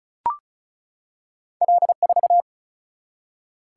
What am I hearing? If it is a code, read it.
Morse code, heard as a keyed tone, about 36 wpm, 700 Hz. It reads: L4
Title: Single-Letter Single-Number 35wpm